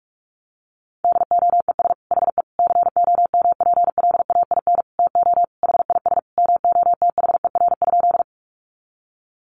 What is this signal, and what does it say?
Morse code, heard as a keyed tone, about 44 wpm, 700 Hz. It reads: DOES HE COMPLAIN TO HIS MOTHER?
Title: Sentences from Top 1300 Words 45wpm